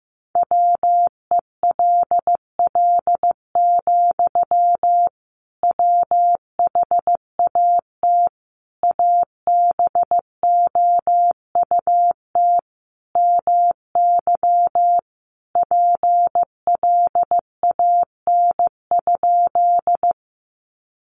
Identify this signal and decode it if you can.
Morse code, heard as a keyed tone, about 15 wpm, 700 Hz. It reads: WELL, WHAT ABOUT MY PLAN?
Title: Sentences from Top 500 Words 15wpm